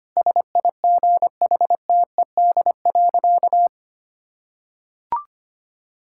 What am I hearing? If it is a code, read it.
Morse code, heard as a keyed tone, about 25 wpm, 700 Hz. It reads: SIGHTED.
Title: Pangram Challenge v2 - 25wpm